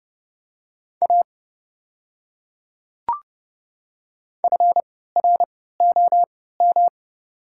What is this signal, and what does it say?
Morse code, heard as a keyed tone, about 29 wpm, spaced as for 17 wpm, 700 Hz. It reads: A FROM
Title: 3x Character Spacing - Top 100 Words - 30wpm